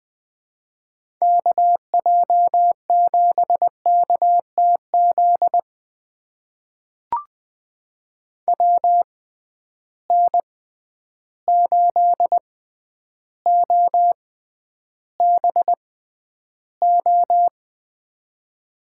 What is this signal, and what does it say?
Morse code, heard as a keyed tone, about 21 wpm, spaced as for 10 wpm, 700 Hz. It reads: KJ7KTZ WN8OBO
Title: USA Call Signs - 6 Characters - 6x Character Spacing - 20wpm